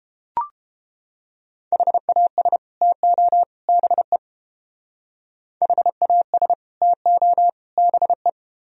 Morse code, heard as a keyed tone, about 30 wpm, 700 Hz. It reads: HAS TO BE HAS TO BE